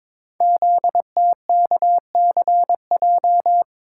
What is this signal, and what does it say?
Morse code, heard as a keyed tone, about 22 wpm, 700 Hz. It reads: ZTKCJ
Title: Code Groups - 5 Letters - 22wpm